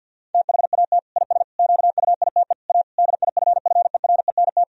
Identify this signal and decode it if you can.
Morse code, heard as a keyed tone, about 50 wpm, 700 Hz. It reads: THAT IS QUITE A DIFFERENT
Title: Sentences from Top 1000 Words - Review Part 3 - 50wpm